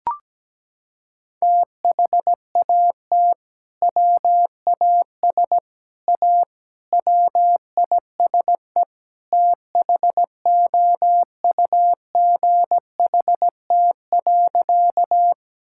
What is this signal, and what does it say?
Morse code, heard as a keyed tone, about 17 wpm, 700 Hz. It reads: THAT WAS A WISE THOUGHT.
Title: Sentences from Top 1600 Words 17wpm